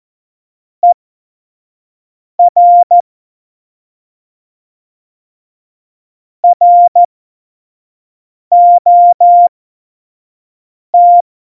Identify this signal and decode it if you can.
Morse code, heard as a keyed tone, about 14 wpm, spaced as for 5 wpm, 700 Hz. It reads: ER ROT
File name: Morsekurs / Witze / Witz 20. November 2018, 25 F70